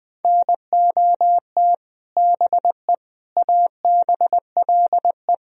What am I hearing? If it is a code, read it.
Morse code, heard as a keyed tone, about 21 wpm, 700 Hz. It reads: NOT BE ABLE
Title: Speed-Racing - Sets of 4 Words 20wpm